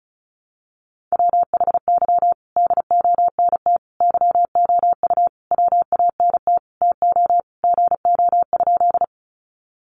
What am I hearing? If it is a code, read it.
Morse code, heard as a keyed tone, about 34 wpm, 700 Hz. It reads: WHY DONT YOU WANT TO GO?
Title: Sentences from Top 300 Words 35wpm